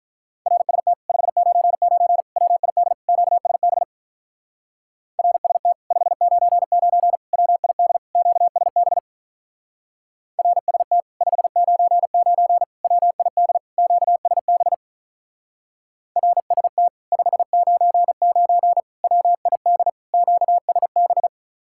Morse code, heard as a keyed tone, about 44 wpm, 700 Hz. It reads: RST 599 WID QSB RST 599 WID QSB RST 599 WID QSB RST 599 WID QSB
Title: Speed-Racing Common QSO Phrases - 35wpm